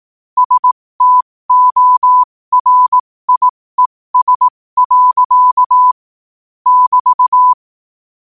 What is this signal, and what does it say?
Morse code, heard as a keyed tone, about 18 wpm, spaced as for 15 wpm, 1000 Hz. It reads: STORIES. <BT>
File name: qotdfiles / 15wpm